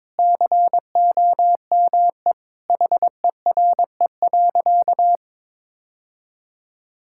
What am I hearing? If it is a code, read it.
Morse code, heard as a keyed tone, about 22 wpm, 700 Hz. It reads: COME HERE.
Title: Sentences from Top 1600 Words 22wpm